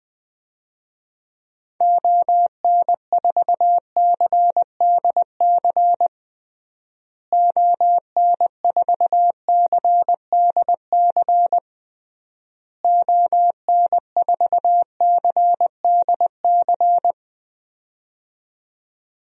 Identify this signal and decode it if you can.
Morse code, heard as a keyed tone, about 20 wpm, 700 Hz. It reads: ON4CDC ON4CDC ON4CDC
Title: DXCC-100 Call Signs - 6 Characters - Repeated 3x - 20wpm